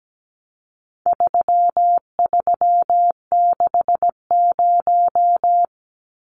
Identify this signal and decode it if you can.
Morse code, heard as a keyed tone, about 17 wpm, 700 Hz. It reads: 3360
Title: Numbers - 4-Digit 17wpm